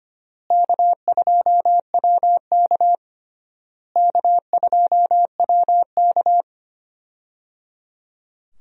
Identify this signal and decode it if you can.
Morse code, heard as a keyed tone, about 25 wpm, 700 Hz. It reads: K2WK K2WK
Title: USA Call Signs - 4 Characters - Repeated 3x - 25wpm